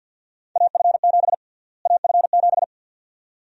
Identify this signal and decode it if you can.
Morse code, heard as a keyed tone, about 46 wpm, 700 Hz. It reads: AF7 AF7
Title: USA Call Sign Prefixes - Repeated 2x - 50wpm